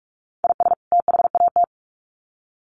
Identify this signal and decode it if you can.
Morse code, heard as a keyed tone, about 44 wpm, 700 Hz. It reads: IS THAT
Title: Sets of 2 Words 45wpm